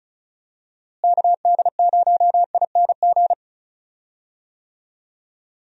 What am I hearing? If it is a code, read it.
Morse code, heard as a keyed tone, about 36 wpm, 700 Hz. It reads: KD0ING
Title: USA Call Signs - 6 Characters - Repeated 2x - 35wpm